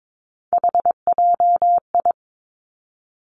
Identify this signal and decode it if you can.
Morse code, heard as a keyed tone, about 22 wpm, 700 Hz. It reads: HJI
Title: Three Challenging Letter-Number Combinations 22wpm